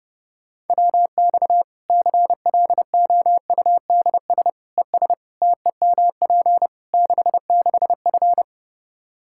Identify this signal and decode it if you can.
Morse code, heard as a keyed tone, about 32 wpm, 700 Hz. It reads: WX CLOUDS ES TEMP 66F
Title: Practice QSO - Exchange 2 - Info - 30wpm